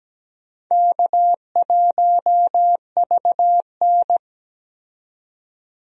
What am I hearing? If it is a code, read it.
Morse code, heard as a keyed tone, about 17 wpm, 700 Hz. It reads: K1VN
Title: USA Call Signs - 4 Characters - 17wpm